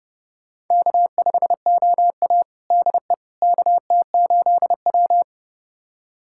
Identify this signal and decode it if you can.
Morse code, heard as a keyed tone, about 32 wpm, 700 Hz. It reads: K5OA DE KT8W
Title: USA Callsign DE Callsign - 4 Characters - 30wpm